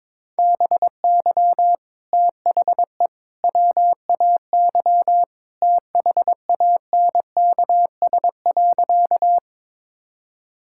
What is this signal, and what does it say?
Morse code, heard as a keyed tone, about 21 wpm, 700 Hz. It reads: BY THE WAY THANKS.
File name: practice / 22wpm / Sentences with 3-Word Phrases - 22wpm